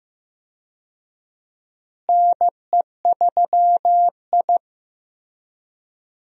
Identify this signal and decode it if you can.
Morse code, heard as a keyed tone, about 15 wpm, 700 Hz. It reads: NE3I
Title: USA Call Signs - 4 Characters - Repeated 3x - 15wpm